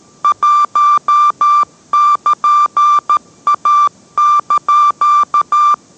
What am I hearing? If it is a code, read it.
Morse code, heard as a keyed tone, about 13 wpm, 1180 Hz. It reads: 1<KN>A)